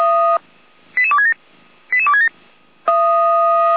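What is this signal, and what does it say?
Five-tone selective call (ZVEI), heard as quick runs of five tones, tones of about 80 ms. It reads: 88267 88267
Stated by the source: ZVEI five-tone call 88211 received by SDR.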